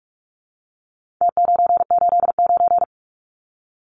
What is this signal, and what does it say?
Morse code, heard as a keyed tone, about 44 wpm, 700 Hz. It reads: T989
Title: Numbers - Contest Style - 4-Digit 45wpm